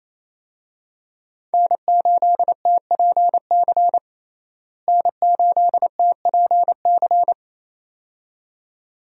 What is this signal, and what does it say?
Morse code, heard as a keyed tone, about 28 wpm, 700 Hz. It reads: N8TPC N8TPC
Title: USA Call Signs - 5 Characters - Repeated 2x - 28wpm